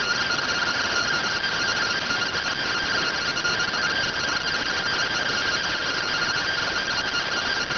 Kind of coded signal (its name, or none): two-tone teleprinter signal